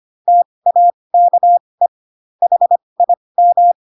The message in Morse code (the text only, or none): TAKE HIM